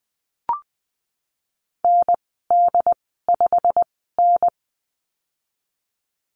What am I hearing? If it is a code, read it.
Morse code, heard as a keyed tone, about 20 wpm, spaced as for 15 wpm, 700 Hz. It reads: ND5N
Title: 128 Lesson 21g - Number 6 - Call Signs 20wpm